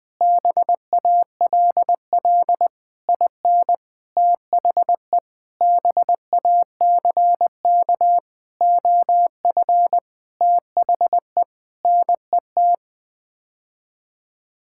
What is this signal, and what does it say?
Morse code, heard as a keyed tone, about 20 wpm, 700 Hz. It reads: BALL IN THE BACK OF THE NET
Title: Sets of 9 Words - 20wpm